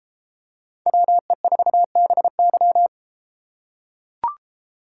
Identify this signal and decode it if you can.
Morse code, heard as a keyed tone, about 30 wpm, 700 Hz. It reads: WE4BY